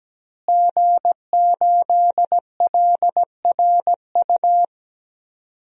Morse code, heard as a keyed tone, about 17 wpm, 700 Hz. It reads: G8LRU
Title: DXCC-100 Call Signs - 5 Characters - Repeated 3x - 17wpm